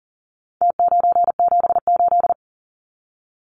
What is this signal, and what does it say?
Morse code, heard as a keyed tone, about 40 wpm, 700 Hz. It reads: T978
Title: Numbers - Contest Style - 4-Digit 40wpm